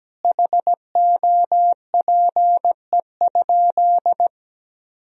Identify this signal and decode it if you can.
Morse code, heard as a keyed tone, about 17 wpm, 700 Hz. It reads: HOPE?
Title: Sentences from Top 1000 Words - Review Part 3 - 17wpm